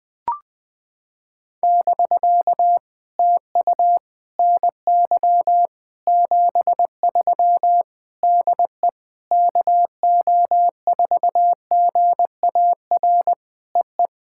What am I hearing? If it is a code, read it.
Morse code, heard as a keyed tone, about 20 wpm, 700 Hz. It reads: <BK> TU NY 73 DE KO4GAR EE
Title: POTA QSO Practice - 20wpm